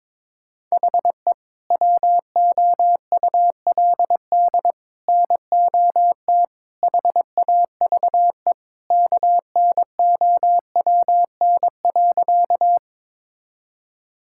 Morse code, heard as a keyed tone, about 22 wpm, 700 Hz. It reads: HE WOULD NOT HAVE KNOWN.
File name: practice / 22wpm / Sentences with 3-Word Phrases - 22wpm